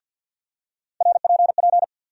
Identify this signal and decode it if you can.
Morse code, heard as a keyed tone, about 52 wpm, 700 Hz. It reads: RPP
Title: USA Call Sign Postfixes - 3 Letters 50wpm